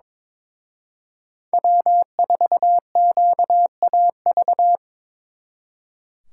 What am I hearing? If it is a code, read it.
Morse code, heard as a keyed tone, about 26 wpm, 700 Hz. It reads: W4QAV
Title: Speed Racing - USA 1x3 Call Signs 22wpm